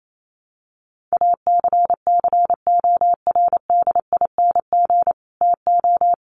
Morse code, heard as a keyed tone, about 28 wpm, 700 Hz. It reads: ACCORDING TO
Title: Sentences from Top 800 Words 28wpm